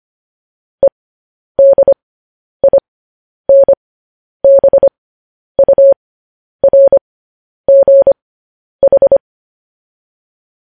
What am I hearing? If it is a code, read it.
Morse code, heard as a keyed tone, about 25 wpm, spaced as for 10 wpm, 550 Hz. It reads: EDINBURGH
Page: news headlines in morse code at 10 wpm